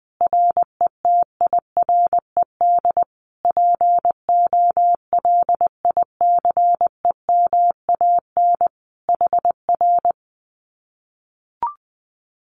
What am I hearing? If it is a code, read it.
Morse code, heard as a keyed tone, about 20 wpm, 700 Hz. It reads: RETIRED POLICEMAN HR
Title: Common QSO Phrases Encore 20wpm